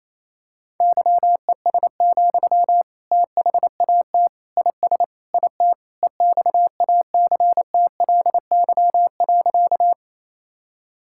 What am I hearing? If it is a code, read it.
Morse code, heard as a keyed tone, about 28 wpm, 700 Hz. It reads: YES, THAT IS IT EXACTLY.
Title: Sentences from Top 1000 Words - Review Part 6 - 28wpm